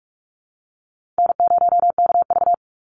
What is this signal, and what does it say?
Morse code, heard as a keyed tone, about 44 wpm, 700 Hz. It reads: N0KV
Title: Call Signs Repeated 3x - 45wpm